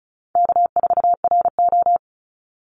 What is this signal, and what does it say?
Morse code, heard as a keyed tone, about 34 wpm, 700 Hz. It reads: K4RO
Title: Easy Call Signs Repeated 3x - 35wpm